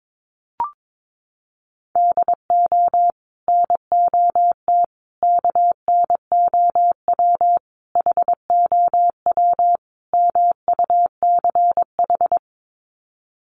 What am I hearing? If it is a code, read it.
Morse code, heard as a keyed tone, about 22 wpm, 700 Hz. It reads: DO NOT KNOW HOW MUCH
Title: Sets of 5 Words Encore 22wpm